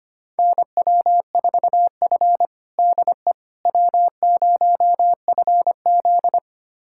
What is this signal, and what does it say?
Morse code, heard as a keyed tone, about 25 wpm, 700 Hz. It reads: NW4F DE W0FZ
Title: USA Callsign DE Callsign - 4 Characters - 25wpm